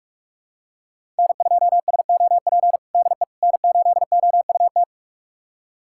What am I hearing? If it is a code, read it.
Morse code, heard as a keyed tone, about 46 wpm, 700 Hz. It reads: N2SOP DE N8OUT
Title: USA Callsign DE Callsign - 5 Characters - 45wpm